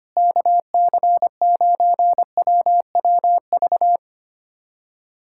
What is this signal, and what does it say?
Morse code, heard as a keyed tone, about 25 wpm, 700 Hz. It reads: KC9WWV
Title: USA Callsign DE Callsign 25wpm